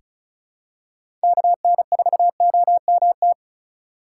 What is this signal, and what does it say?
Morse code, heard as a keyed tone, about 36 wpm, 700 Hz. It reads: KN4OMT
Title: USA Call Signs - 6 Characters - 3x Character Spacing - 35wpm